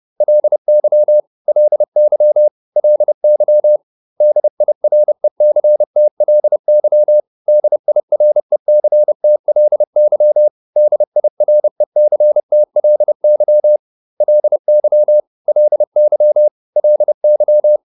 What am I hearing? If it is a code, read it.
Morse code, heard as a keyed tone, about 30 wpm, 600 Hz. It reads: LY LY LY DIRECTLY DIRECTLY DIRECTLY LY LY LY